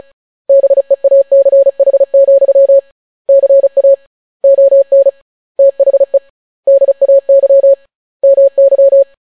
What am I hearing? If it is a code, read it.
Morse code, heard as a keyed tone, about 36 wpm, 550 Hz. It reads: BEACH, CA ON THE DAY MY